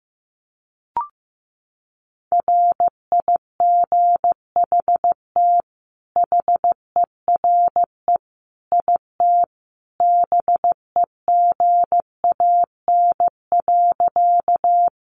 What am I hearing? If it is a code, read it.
Morse code, heard as a keyed tone, about 15 wpm, 700 Hz. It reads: RIGHT HERE IT BEGAN.
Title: Sentences from Top 300 Words 15wpm